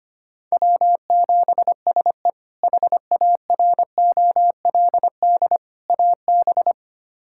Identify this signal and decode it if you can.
Morse code, heard as a keyed tone, about 25 wpm, 700 Hz. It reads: W7SE HAROLD AB
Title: K1USN SST QSO Practice - 25wpm